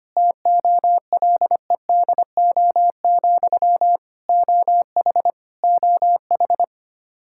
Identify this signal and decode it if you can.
Morse code, heard as a keyed tone, about 25 wpm, 700 Hz. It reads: TOLEDO, OH OH
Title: Practice QSO - Exchange 1 - Beginning - 25wpm